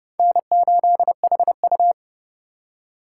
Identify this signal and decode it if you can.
Morse code, heard as a keyed tone, about 39 wpm, 700 Hz. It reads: N8HU